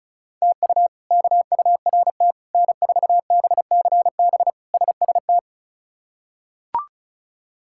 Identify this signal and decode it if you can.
Morse code, heard as a keyed tone, about 36 wpm, 700 Hz. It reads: TU KURT N4BCB SST
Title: K1USN SST QSO Practice - 35wpm